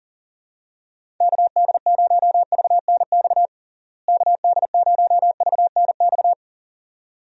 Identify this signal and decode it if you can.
Morse code, heard as a keyed tone, about 40 wpm, 700 Hz. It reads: KD0VNX KD0VNX
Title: USA Call Signs - 6 Characters - Repeated 2x - 40wpm